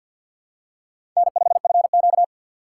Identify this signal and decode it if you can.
Morse code, heard as a keyed tone, about 53 wpm, 700 Hz. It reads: N5FQ
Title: USA Call Signs - 6x Character Spacing - 50wpm